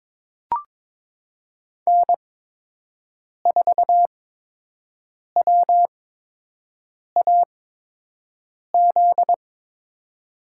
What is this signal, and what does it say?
Morse code, heard as a keyed tone, about 22 wpm, spaced as for 6 wpm, 700 Hz. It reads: N4WAZ
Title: USA Call Signs - 8x Character Spacing - 22wpm